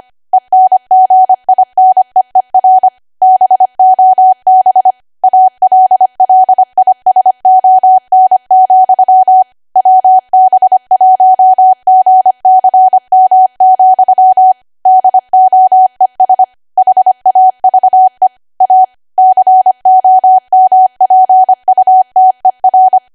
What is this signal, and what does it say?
Morse code, heard as a keyed tone, about 25 wpm, 750 Hz. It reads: ENGINEER BOB ALLISON, WB1GCM, DOES HAVE A COMPUTER
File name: Archive / 25 WPM / 25WPM